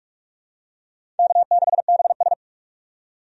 Morse code, heard as a keyed tone, about 46 wpm, 700 Hz. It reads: K6BS
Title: USA Call Signs - 4 Characters - 45wpm